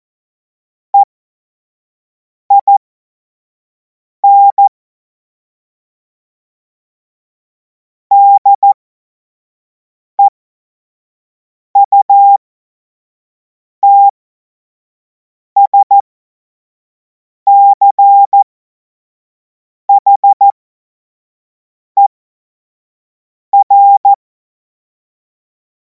Morse code, heard as a keyed tone, about 14 wpm, spaced as for 5 wpm, 800 Hz. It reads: EIN DEUTSCHER
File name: Morsekurs / Witze / Witz 14. Juli 2026 25 F70